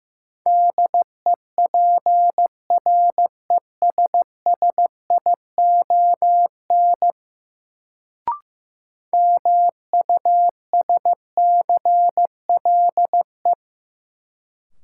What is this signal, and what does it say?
Morse code, heard as a keyed tone, about 15 wpm, 700 Hz. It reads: DEPRESSION MUSCLE
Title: Sentences from Top 1300 Words - New Word Drill Alt - 15wpm